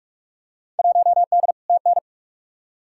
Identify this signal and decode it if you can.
Morse code, heard as a keyed tone, about 45 wpm, 700 Hz. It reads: 1D TN
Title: Field Day Exchanges - 45wpm